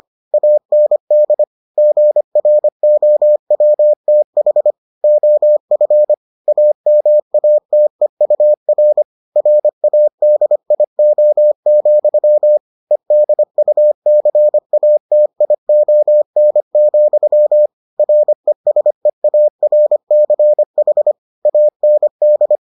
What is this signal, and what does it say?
Morse code, heard as a keyed tone, about 25 wpm, 600 Hz. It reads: AND GROWTH OF AMATEUR RADIO, EDUCATION, RESEARCH AND